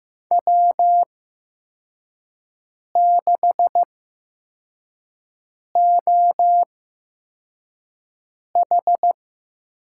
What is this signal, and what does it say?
Morse code, heard as a keyed tone, about 15 wpm, spaced as for 4 wpm, 700 Hz. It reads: W6OH
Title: USA Call Signs - 4 Characters - 8x Character Spacing - 15wpm